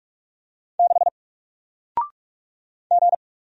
Morse code, heard as a keyed tone, about 44 wpm, 700 Hz. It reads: 6 G
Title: Single Character (Letter:Number:Punctuation:Prosign) 45wpm